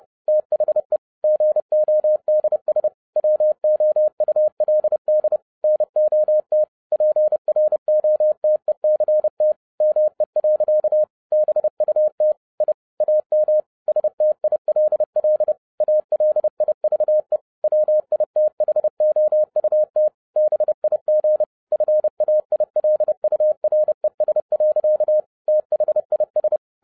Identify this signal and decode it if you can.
Morse code, heard as a keyed tone, about 30 wpm, 610 Hz. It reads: THE GODS WOULD NOT PROTECT ME. BUT I AM STILL ALIVE WITHOUT BIG FAILURES. THIS
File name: 2011-07-mp3 / n60 BACK FROM A BIG ACCIDENT 30wpm